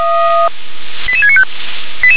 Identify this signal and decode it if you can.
Five-tone selective call (ZVEI), heard as quick runs of five tones, tones of about 70 ms. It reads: 88695